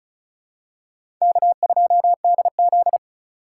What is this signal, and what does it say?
Morse code, heard as a keyed tone, about 36 wpm, 700 Hz. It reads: K2DZ
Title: USA Call Signs - 4x Character Spacing - 35wpm